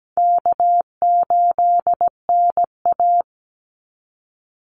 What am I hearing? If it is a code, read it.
Morse code, heard as a keyed tone, about 17 wpm, 700 Hz. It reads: K8NA
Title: Easy Call Signs Repeated 2x - 17wpm